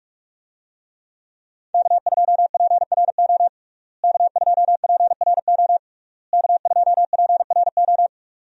Morse code, heard as a keyed tone, about 46 wpm, 700 Hz. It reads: K2PRO K2PRO K2PRO
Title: USA Call Signs - 5 Characters - Repeated 3x - 45wpm